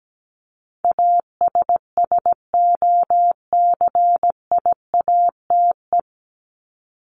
Morse code, heard as a keyed tone, about 17 wpm, 700 Hz. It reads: ASSOCIATE
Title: Top 1751-2000 Words 17wpm